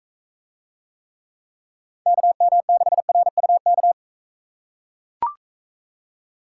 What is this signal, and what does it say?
Morse code, heard as a keyed tone, about 43 wpm, 700 Hz. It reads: KM6RUK